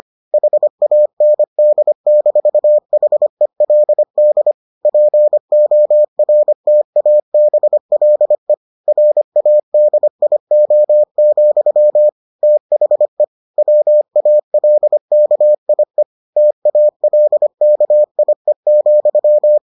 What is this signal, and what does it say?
Morse code, heard as a keyed tone, about 25 wpm, 600 Hz. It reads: HAND-HELD PORTABLE RADIO, THE WALKIE TALKIE,